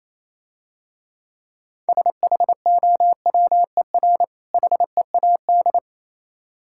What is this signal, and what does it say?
Morse code, heard as a keyed tone, about 28 wpm, 700 Hz. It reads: SHOWER HEAD